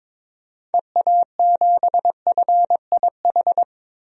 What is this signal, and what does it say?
Morse code, heard as a keyed tone, about 22 wpm, 700 Hz. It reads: EA7FIH
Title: DXCC-100 Call Signs - 6 Characters - 22wpm